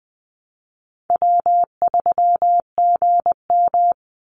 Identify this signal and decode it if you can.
Morse code, heard as a keyed tone, about 20 wpm, 700 Hz. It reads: W3GM
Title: USA Call Signs 20wpm